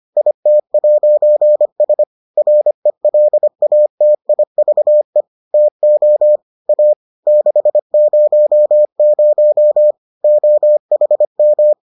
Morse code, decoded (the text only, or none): IT'S RELATIVE TO A 600 OHM